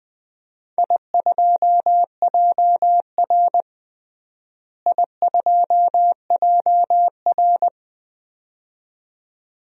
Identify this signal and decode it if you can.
Morse code, heard as a keyed tone, about 20 wpm, 700 Hz. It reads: I2JR I2JR
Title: DXCC-100 Call Signs - 4 Characters - Repeated 2x - 20wpm